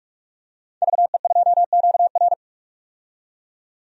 Morse code, heard as a keyed tone, about 46 wpm, 700 Hz. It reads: VE2QR
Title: DXCC-100 Call Signs - 5 Characters - 45wpm